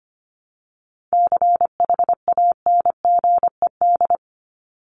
Morse code, heard as a keyed tone, about 25 wpm, 700 Hz. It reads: CHANGED
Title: ED Suffix - Top 100 Words with Base Words 25wpm